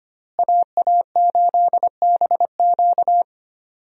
Morse code, heard as a keyed tone, about 25 wpm, 700 Hz. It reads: AA8BQ
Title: USA Call Signs - 5 Characters - Repeated 3x - 25wpm